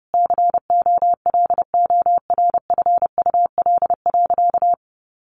Morse code, heard as a keyed tone, about 30 wpm, 700 Hz. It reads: COLORFUL.
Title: Easy Phrases 30wpm